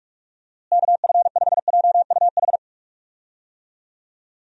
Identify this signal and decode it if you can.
Morse code, heard as a keyed tone, about 46 wpm, 700 Hz. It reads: KF5JUH
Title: USA Call Signs - 6 Characters - Repeated 2x - 45wpm